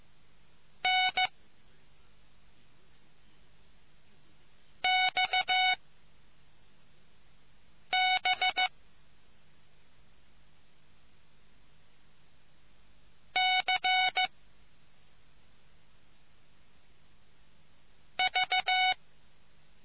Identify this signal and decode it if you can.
Morse code, heard as a keyed tone, about 14 wpm, 740 Hz. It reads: N X B C V